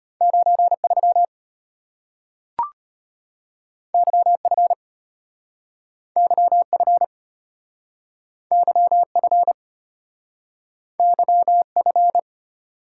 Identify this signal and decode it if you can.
Morse code, heard as a keyed tone, about 34 wpm, 700 Hz. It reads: 93 YF YF YF YF